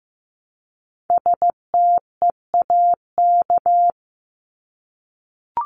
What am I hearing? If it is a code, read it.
Morse code, heard as a keyed tone, about 15 wpm, 700 Hz. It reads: STEAK
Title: Top 1251-1500 Words 15wpm